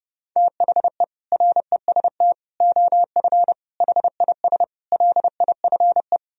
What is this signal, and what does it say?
Morse code, heard as a keyed tone, about 32 wpm, 700 Hz. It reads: THE REST OF HIS LIFE
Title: Speed-Racing - Sets of 5 Words 30wpm